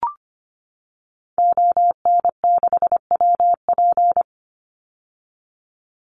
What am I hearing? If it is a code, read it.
Morse code, heard as a keyed tone, about 25 wpm, 700 Hz. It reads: ON6WP
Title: Call Signs - 25wpm